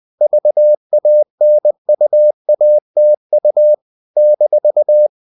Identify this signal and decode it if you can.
Morse code, heard as a keyed tone, about 20 wpm, 600 Hz. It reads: VANUATU -
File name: CW / Practice / NationalCapitals20